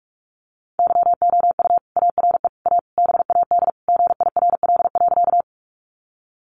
Morse code, heard as a keyed tone, about 44 wpm, 700 Hz. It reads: YOU ARE A BAD GIRL.
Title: Sentences from Top 700 Words 45wpm